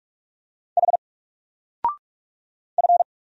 Morse code, heard as a keyed tone, about 46 wpm, 700 Hz. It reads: H F
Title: Single Letter-Number 45wpm